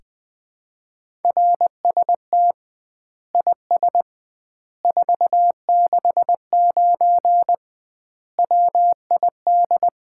Morse code, heard as a keyed tone, about 21 wpm, 700 Hz. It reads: RST IS 469 WID
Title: Common QSO Phrases - 2x Word-Spacing - 20wpm